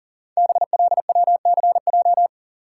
Morse code, heard as a keyed tone, about 42 wpm, 690 Hz. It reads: BLWCJ
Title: Code Groups - 5 Letters - 40wpm